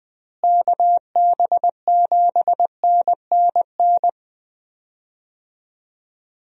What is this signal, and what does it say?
Morse code, heard as a keyed tone, about 20 wpm, 700 Hz. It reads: KB7NNN
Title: USA Call Signs - 6 Characters - Repeated 2x - 20wpm